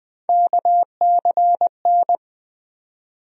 Morse code, heard as a keyed tone, about 20 wpm, 700 Hz. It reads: KCN
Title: USA Call Sign Postfixes - 3 Letters 20wpm